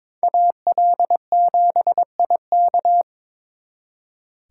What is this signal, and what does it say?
Morse code, heard as a keyed tone, about 26 wpm, 700 Hz. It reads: AL7IK
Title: Speed Racing - USA 2x2 Call Signs 22wpm